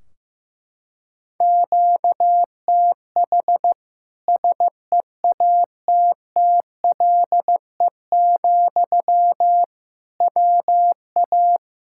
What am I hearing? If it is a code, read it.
Morse code, heard as a keyed tone, about 15 wpm, 700 Hz. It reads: QTH SEATTLE, WA
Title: Practice QSO - Exchange 1 - Beginning - 15wpm